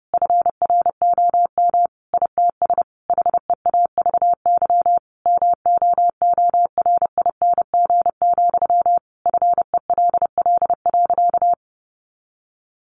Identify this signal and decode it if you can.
Morse code, heard as a keyed tone, about 30 wpm, 700 Hz. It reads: FROM ITS HEAVY MOORING, FELL.
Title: A Sound Of Thunder 30wpm